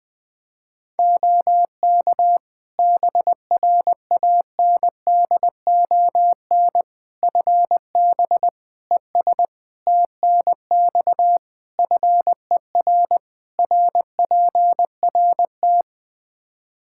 Morse code, heard as a keyed tone, about 20 wpm, 700 Hz. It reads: OK BRANDON FB ES TNX FER RPRT
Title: Practice QSO - Exchange 2 - Info - 20wpm